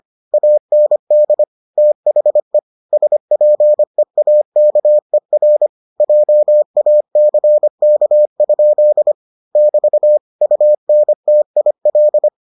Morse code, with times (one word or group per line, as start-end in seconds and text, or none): 0.3-1.4 AND
1.8-2.6 THE
2.9-5.7 SPEAKER
6.0-9.1 JACK?
9.6-10.2 <BT>
10.4-12.3 UNTIL